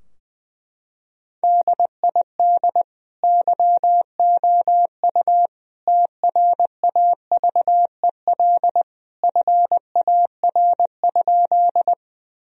Morse code, heard as a keyed tone, about 21 wpm, 700 Hz. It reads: DID YOU TRAVEL FAR?
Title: Sentences from Top 1000 Words - Review Part 2 - 20wpm